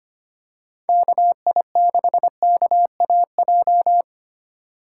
Morse code, heard as a keyed tone, about 25 wpm, 700 Hz. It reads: KI6KAJ
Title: USA Call Signs - 3x Character Spacing - 25wpm